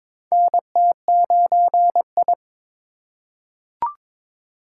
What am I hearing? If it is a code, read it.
Morse code, heard as a keyed tone, about 22 wpm, 700 Hz. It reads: NT9I
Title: USA Call Signs - 4 Characters - 2x Character Spacing - 22wpm